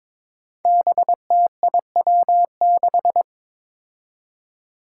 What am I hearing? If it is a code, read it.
Morse code, heard as a keyed tone, about 22 wpm, 700 Hz. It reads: BTIW6
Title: Five Challenging Letter-Number Combinations 22wpm